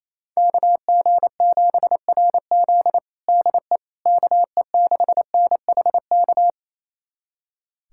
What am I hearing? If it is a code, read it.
Morse code, heard as a keyed tone, about 28 wpm, 700 Hz. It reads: KG7RZ DE KE6NHK
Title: USA Callsign DE Callsign 28wpm